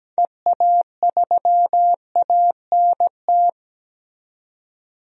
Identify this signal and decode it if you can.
Morse code, heard as a keyed tone, about 17 wpm, 700 Hz. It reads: EA3ANT
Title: DXCC-100 Call Signs - 6 Characters - 17wpm